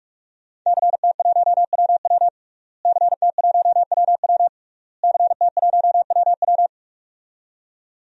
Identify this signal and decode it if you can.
Morse code, heard as a keyed tone, about 44 wpm, 700 Hz. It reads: CT1WW CT1WW CT1WW
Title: DXCC-100 Call Signs - 5 Characters - Repeated 3x - 45wpm